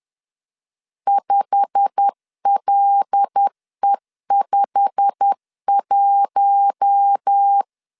Morse code, heard as a keyed tone, about 10 wpm, 800 Hz. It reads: 5LE51
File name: LEZIONI / 8 YJ45 / 8 YJ45 10wpm 2549